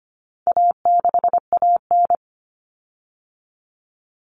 Morse code, heard as a keyed tone, about 25 wpm, 700 Hz. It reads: A6AN
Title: Common Cut Numbers - 4-Digit 25wpm